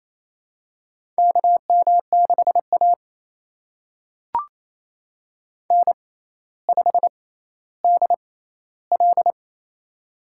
Morse code, heard as a keyed tone, about 28 wpm, spaced as for 14 wpm, 700 Hz. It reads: KM6A N5DL
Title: USA Call Signs - 4 Characters - 6x Character Spacing - 28wpm